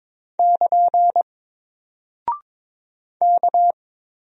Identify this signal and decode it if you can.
Morse code, heard as a keyed tone, about 22 wpm, 700 Hz. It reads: <KN> K
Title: Single Character (Letter:Number:Punctuation:Prosign) Alt 22wpm